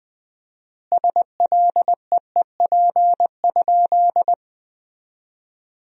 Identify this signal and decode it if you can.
Morse code, heard as a keyed tone, about 20 wpm, 700 Hz. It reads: SLEEP?
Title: 4x Word-Spacing - Sentences from Top 500 Words - 20wpm